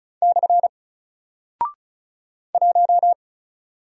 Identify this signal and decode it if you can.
Morse code, heard as a keyed tone, about 36 wpm, 700 Hz. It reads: / 1
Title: Single Character (Letter:Number:Punctuation:Prosign) Alt 35wpm